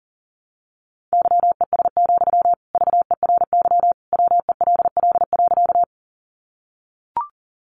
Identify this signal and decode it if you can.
Morse code, heard as a keyed tone, about 40 wpm, 700 Hz. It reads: YES, VERY WELL.
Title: Sentences from Top 500 Words 40wpm